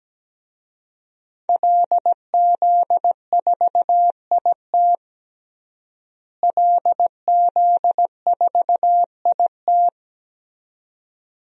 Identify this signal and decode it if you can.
Morse code, heard as a keyed tone, about 17 wpm, 700 Hz. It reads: LZ4IT LZ4IT
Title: DXCC-100 Call Signs - 5 Characters - Repeated 2x - 17wpm